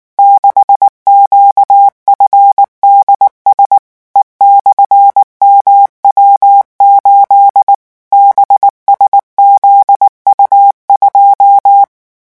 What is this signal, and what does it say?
Morse code, heard as a keyed tone, about 19 wpm, 800 Hz. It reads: 6QFDS E/MW8 BSZU2
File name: morse mp3 / revision12-vitesse 18 mots